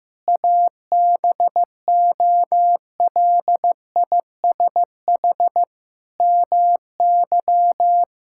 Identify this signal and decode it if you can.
Morse code, heard as a keyed tone, about 15 wpm, 700 Hz. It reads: ABOLISH MY
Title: Pangram Challenge v2 - 15wpm